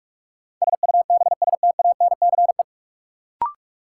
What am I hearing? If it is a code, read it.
Morse code, heard as a keyed tone, about 43 wpm, 700 Hz. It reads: SUBSTANCE
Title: Sentences from Top 1500 Words - New Word Drill Alt - 45wpm